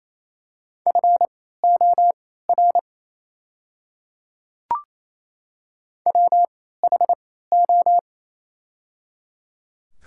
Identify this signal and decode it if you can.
Morse code, heard as a keyed tone, about 28 wpm, spaced as for 16 wpm, 700 Hz. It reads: FOR WHO
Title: 3x Character Spacing - Top 300 Words - 28wpm